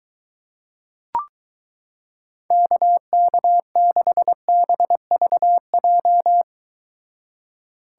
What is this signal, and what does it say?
Morse code, heard as a keyed tone, about 23 wpm, 700 Hz. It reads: KK6BVJ